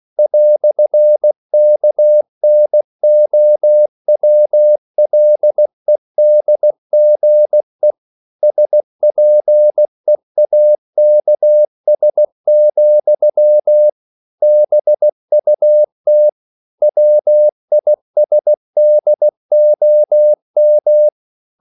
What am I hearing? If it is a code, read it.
Morse code, heard as a keyed tone, about 16 wpm, 600 Hz. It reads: "KNOWLEDGE SPEAKS, BUT WISDOM
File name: CW / Practice / EnglishQuotations0216